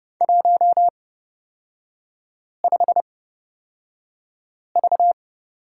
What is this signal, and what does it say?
Morse code, heard as a keyed tone, about 30 wpm, spaced as for 5 wpm, 700 Hz. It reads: 15V
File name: Lesson 37e - Letter x - Three Characters 30wpm